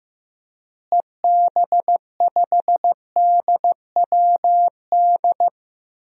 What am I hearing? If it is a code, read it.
Morse code, heard as a keyed tone, about 15 wpm, 700 Hz. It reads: EB5DWD
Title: DXCC-100 Call Signs - 6 Characters - Repeated 3x - 15wpm